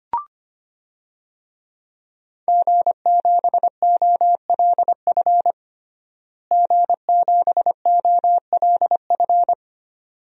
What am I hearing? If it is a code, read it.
Morse code, heard as a keyed tone, about 25 wpm, 700 Hz. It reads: G7OLF G7OLF
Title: DXCC-100 Call Signs - 5 Characters - Repeated 2x - 25wpm